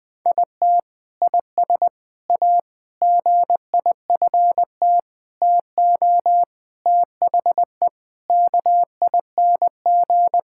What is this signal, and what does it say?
Morse code, heard as a keyed tone, about 21 wpm, 700 Hz. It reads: IT IS A GIFT TO THE KING
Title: Sentences from Top 1000 Words - Review Part 6 - 20wpm